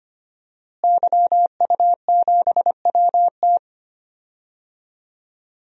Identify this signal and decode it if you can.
Morse code, heard as a keyed tone, about 25 wpm, 700 Hz. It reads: YU7WT
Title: DXCC-100 Call Signs - 5 Characters - Repeated 2x - 25wpm